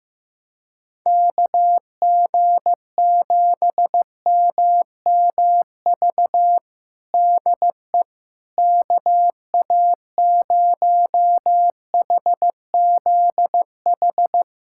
Morse code, heard as a keyed tone, about 15 wpm, 700 Hz. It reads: KG7MMV DE KA0HZH
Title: USA Callsign DE Callsign 15wpm